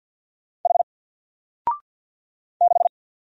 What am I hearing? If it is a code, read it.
Morse code, heard as a keyed tone, about 50 wpm, 700 Hz. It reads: H 6
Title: Single Character (Letter:Number:Punctuation:Prosign) 50wpm